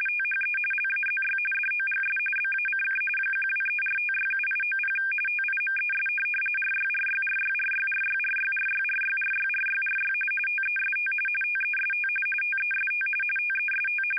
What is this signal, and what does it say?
RTTY, about 45.45 baud. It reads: THE LAZY DOGS BACK 1234567890 TIMES RYRYRYRYRYRYRYRYRYRYRY TEST TEST TEST TEST
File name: Audio Files / All Good Men 60 wpm RTTY 850 Hz